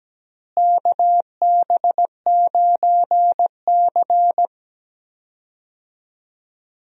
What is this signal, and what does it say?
Morse code, heard as a keyed tone, about 17 wpm, 700 Hz. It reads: KB9C
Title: USA Call Signs - 4 Characters - Repeated 3x - 17wpm